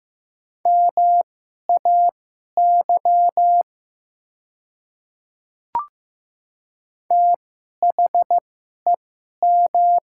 Morse code, heard as a keyed tone, about 15 wpm, spaced as for 11 wpm, 700 Hz. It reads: MAY THEM
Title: 2x Character Spacing - Top 100 Words - 15wpm